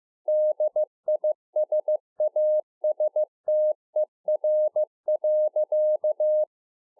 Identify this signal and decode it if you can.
Morse code, heard as a keyed tone, about 15 wpm, 610 Hz. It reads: DISASTER.
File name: 2011-04-mp3 / THE 3.11 6 15WPM